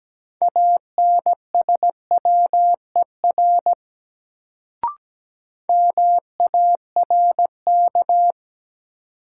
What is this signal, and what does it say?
Morse code, heard as a keyed tone, about 17 wpm, 700 Hz. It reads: ANSWER MARK
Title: Sentences from Top 300 Words - New Word Drill - 17wpm